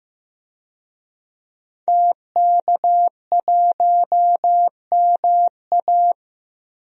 Morse code, heard as a keyed tone, about 15 wpm, 700 Hz. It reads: TK1MA
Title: DXCC-100 Call Signs - 5 Characters - 15wpm